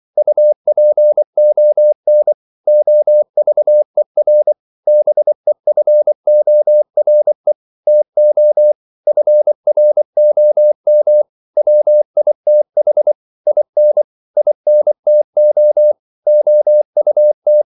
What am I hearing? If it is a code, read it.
Morse code, heard as a keyed tone, about 24 wpm, 600 Hz. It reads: UPON OVER BEFORE TO FROM WITH IN INTO OUT
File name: CW / Practice / 100MostCommonEnglishWords24